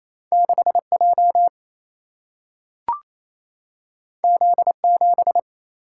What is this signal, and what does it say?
Morse code, heard as a keyed tone, about 28 wpm, 700 Hz. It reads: 6 J Z 7
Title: Single-Letter Single-Number 28wpm